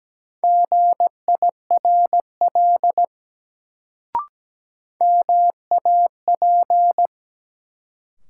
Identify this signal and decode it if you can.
Morse code, heard as a keyed tone, about 17 wpm, 700 Hz. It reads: GIRL MAP
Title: Sentences from Top 400 Words - New Word Drill - 17wpm